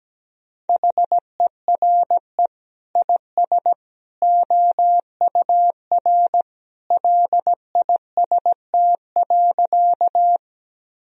Morse code, heard as a keyed tone, about 17 wpm, 700 Hz. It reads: HERE IS OUR LIST.
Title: Sentences from Top 1000 Words - Review Part 2 - 17wpm